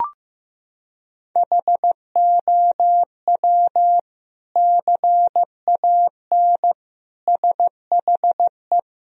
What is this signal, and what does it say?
Morse code, heard as a keyed tone, about 14 wpm, 700 Hz. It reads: HOW CAN SHE
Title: Sentences from Top 1000 Words - Review Part 3 - 15wpm